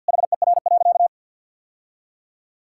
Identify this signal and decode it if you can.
Morse code, heard as a keyed tone, about 50 wpm, 700 Hz. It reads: HER.
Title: Sentences from Top 1000 Words - Review Part 5 - 50wpm